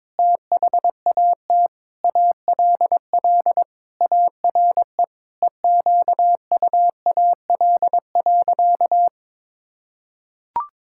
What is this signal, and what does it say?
Morse code, heard as a keyed tone, about 22 wpm, 700 Hz. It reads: THAT ALL ARE EQUAL.
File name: Sentences with 9-Word Phrases - 22wpm